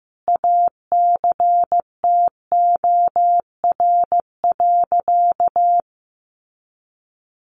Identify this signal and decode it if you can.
Morse code, heard as a keyed tone, about 15 wpm, 700 Hz. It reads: ACTOR.
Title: Sentences from Top 1500 Words 15wpm